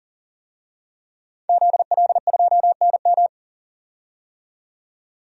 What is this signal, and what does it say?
Morse code, heard as a keyed tone, about 37 wpm, 700 Hz. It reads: ZL2NM
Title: DXCC-100 Call Signs - 5 Characters - 40wpm